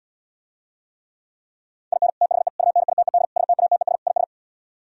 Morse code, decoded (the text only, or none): IN THE UNITED STATES IS